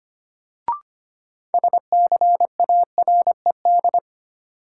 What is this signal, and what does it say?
Morse code, heard as a keyed tone, about 25 wpm, 700 Hz. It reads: SCARED
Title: Sentences from Top 700 Words - New Word Drill - 25wpm